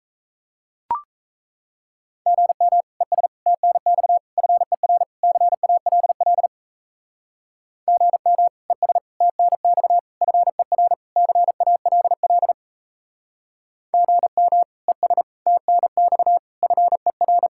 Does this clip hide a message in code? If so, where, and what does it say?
Yes: Morse code from 2.3 to 17.5 s, heard as a keyed tone, about 35 wpm, 700 Hz. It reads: GM ES TNX FER CALL GM ES TNX FER CALL GM ES TNX FER